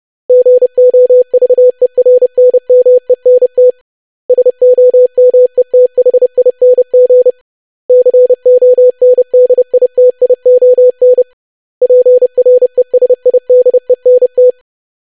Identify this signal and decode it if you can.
Morse code, heard as a keyed tone, about 30 wpm, 500 Hz. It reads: GOVERNMENT SOMETHING CONDITION PRESIDENT
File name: lesson / web / basic500-25wpm